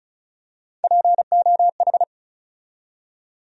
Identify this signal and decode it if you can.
Morse code, heard as a keyed tone, about 34 wpm, 700 Hz. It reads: POH
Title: USA Call Sign Postfixes - 3 Letters 35wpm